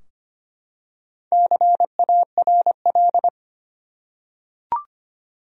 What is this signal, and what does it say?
Morse code, heard as a keyed tone, about 25 wpm, 700 Hz. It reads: CARL
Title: USA Common Names - Max 4 Letters - 25wpm